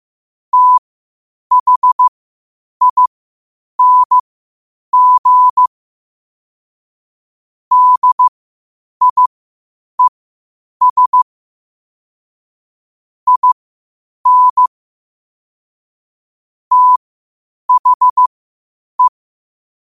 Morse code, heard as a keyed tone, about 15 wpm, spaced as for 8 wpm, 1000 Hz. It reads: THING DIES IN THE
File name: qotdfiles / 8wpm